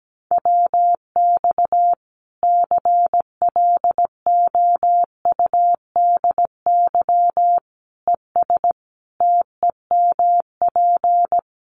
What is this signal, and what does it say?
Morse code, heard as a keyed tone, about 17 wpm, 700 Hz. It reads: WX CLOUDY ES TEMP
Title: Common QSO Phrases 17wpm